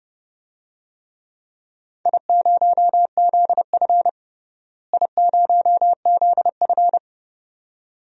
Morse code, heard as a keyed tone, about 30 wpm, 700 Hz. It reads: I0ZF I0ZF
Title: DXCC-100 Call Signs - 4 Characters - Repeated 2x - 30wpm